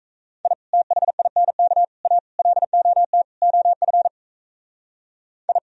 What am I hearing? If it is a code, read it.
Morse code, heard as a keyed tone, about 35 wpm, 700 Hz. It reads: I THINK A LOT OF I
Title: Speed-Racing - Sets of 5 Words 28wpm